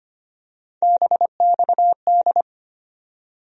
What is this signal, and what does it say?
Morse code, heard as a keyed tone, about 25 wpm, 700 Hz. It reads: BXD
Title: USA Call Sign Postfixes - 3 Letters 25wpm